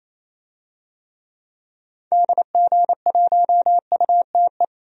Morse code, heard as a keyed tone, about 28 wpm, 700 Hz. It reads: DG1UTE
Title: DXCC-100 Call Signs - 6 Characters - 28wpm